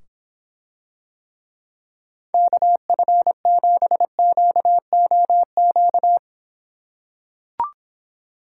Morse code, heard as a keyed tone, about 26 wpm, 700 Hz. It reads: KF7QOQ